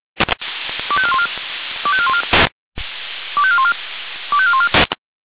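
Five-tone selective call (ZVEI), heard as quick runs of five tones, tones of about 70 ms. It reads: 25414 25414 25414 25414
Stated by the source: ZVEI five-tone selective call 25431 received on an SDR.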